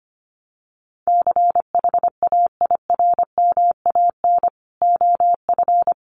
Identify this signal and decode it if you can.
Morse code, heard as a keyed tone, about 25 wpm, 700 Hz. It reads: CHAIRMAN OF
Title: Sets of 5 Words Encore 25wpm